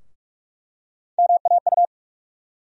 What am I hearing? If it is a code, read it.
Morse code, heard as a keyed tone, about 48 wpm, 700 Hz. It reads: MAU
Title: USA Call Sign Postfixes - 3 Letters 45wpm